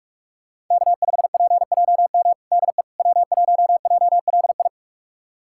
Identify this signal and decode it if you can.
Morse code, heard as a keyed tone, about 41 wpm, 700 Hz. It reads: K5PJM DE W1JLI
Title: USA Callsign DE Callsign - 5 Characters - 45wpm